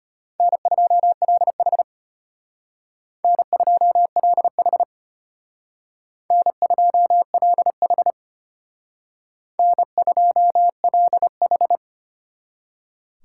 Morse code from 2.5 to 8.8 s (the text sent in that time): N2LH N2LH